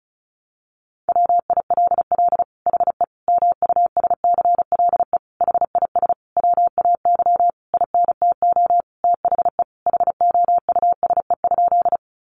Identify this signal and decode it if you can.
Morse code, heard as a keyed tone, about 34 wpm, 700 Hz. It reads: WILL HE MUSCLE HIS WAY INTO THE HOUSE?
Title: Sentences from Top 1300 Words 35wpm